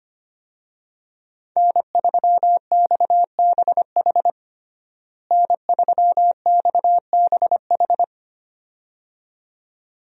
Morse code, heard as a keyed tone, about 25 wpm, 700 Hz. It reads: N3XBH N3XBH
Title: USA Call Signs - 5 Characters - Repeated 2x - 25wpm